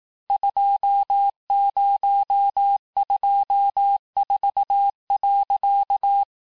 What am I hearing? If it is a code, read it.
Morse code, heard as a keyed tone, about 18 wpm, 800 Hz. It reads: 2024.
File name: cours-du-lundi files / cours-1852-texte-chiffre-vitesse-17MPM